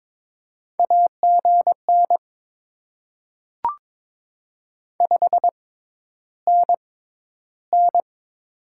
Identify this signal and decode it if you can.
Morse code, heard as a keyed tone, about 22 wpm, spaced as for 11 wpm, 700 Hz. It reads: AGN 5NN
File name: Top 20 QSO Elements - 6x Character Spacing - 22wpm